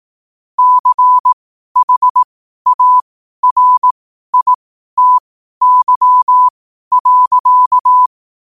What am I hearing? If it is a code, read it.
Morse code, heard as a keyed tone, about 18 wpm, spaced as for 12 wpm, 1000 Hz. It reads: CHARITY.
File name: qotdfiles / 12wpm